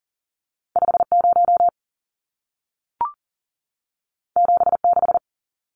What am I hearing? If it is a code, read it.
Morse code, heard as a keyed tone, about 40 wpm, 700 Hz. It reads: 50 76
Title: Numbers - 2-Digit 40wpm